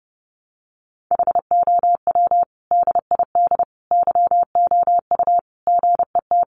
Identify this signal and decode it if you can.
Morse code, heard as a keyed tone, about 30 wpm, 700 Hz. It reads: HOW DID YOU GET
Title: Sentences from Top 200 Words 30wpm